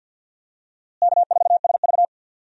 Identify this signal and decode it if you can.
Morse code, heard as a keyed tone, about 51 wpm, 700 Hz. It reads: K4SV
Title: USA Call Signs - 4 Characters - Repeated 2x - 50wpm